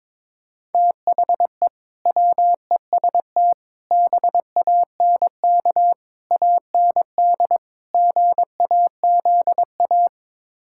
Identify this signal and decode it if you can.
Morse code, heard as a keyed tone, about 22 wpm, 700 Hz. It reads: THE WEST BANK AND GAZA
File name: practice / 22wpm / Build A Phrase - 5-Word Phrases - 22wpm